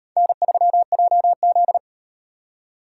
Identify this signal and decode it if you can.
Morse code, heard as a keyed tone, about 38 wpm, 700 Hz. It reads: N3JZ